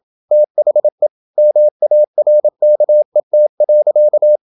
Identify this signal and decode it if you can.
Morse code, heard as a keyed tone, about 27 wpm, 600 Hz. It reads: THE MARKET.
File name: Never 0027wpm